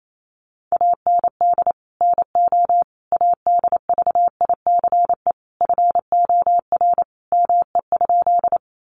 Morse code, heard as a keyed tone, about 28 wpm, 700 Hz. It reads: AND NO ADVICE FOR ME?
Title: Sentences from Top 1100 Words 28wpm